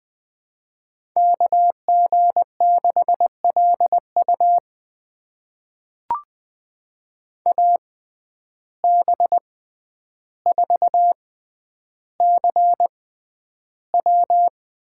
Morse code, heard as a keyed tone, about 20 wpm, spaced as for 10 wpm, 700 Hz. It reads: KG6LU AB4CW
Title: USA Call Signs - 5 Characters - 6x Character Spacing - 20wpm